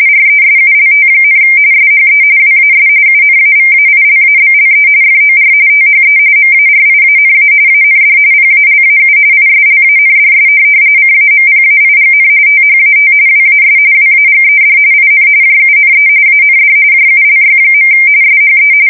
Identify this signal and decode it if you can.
RTTY, about 45.45 baud. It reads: RESULT, SHE WAS UNAVAILABLE FOR THE MARIANA AND PALAU ISLANDS CAMPAIGN THAT WAS WAGED IN JUNE AND JULY. SHE
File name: fsk170 / USS-Massachusetts